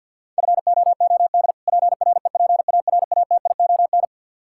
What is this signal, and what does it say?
Morse code, heard as a keyed tone, about 52 wpm, 700 Hz. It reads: FOOD PREPARATION
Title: Practice QSO - Exchange 3 - More Info - 50wpm